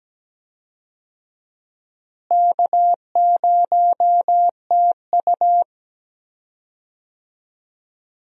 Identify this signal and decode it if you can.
Morse code, heard as a keyed tone, about 17 wpm, 700 Hz. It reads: K0TU
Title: USA Call Signs - 4 Characters - Repeated 2x - 17wpm